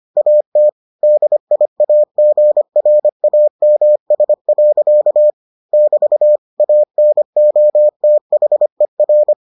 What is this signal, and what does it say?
Morse code, heard as a keyed tone, about 25 wpm, 600 Hz. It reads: AT DIAGRAMS. <BT> ANOTHER